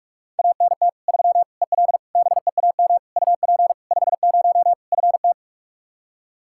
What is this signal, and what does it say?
Morse code, heard as a keyed tone, about 46 wpm, 700 Hz. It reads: ANT 3 EL BEAM UP 50 FT
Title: Practice QSO - Exchange 2 - Info - 45wpm